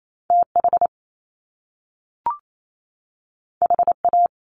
Morse code, heard as a keyed tone, about 28 wpm, 700 Hz. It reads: TH HA
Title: Top Two Letter Combinations in Words 28wpm